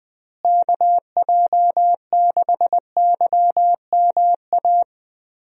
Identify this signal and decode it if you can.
Morse code, heard as a keyed tone, about 20 wpm, 700 Hz. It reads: KJ6YMA
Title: USA Call Signs - 6 Characters - Repeated 3x - 20wpm